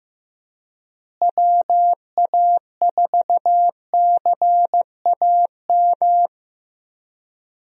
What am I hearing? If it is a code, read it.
Morse code, heard as a keyed tone, about 15 wpm, 700 Hz. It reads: WA4CAM
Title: USA Call Signs - 6 Characters - Repeated 2x - 15wpm